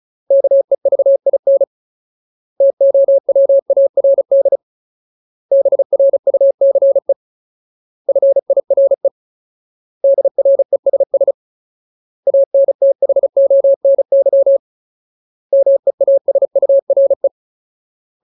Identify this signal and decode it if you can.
Morse code, heard as a keyed tone, about 36 wpm, 550 Hz. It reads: KEVIN TOWARD BRUCE FIRE DRESS ANTHONY MEASURE
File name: Ultimate Word List / 35WPM Ultimate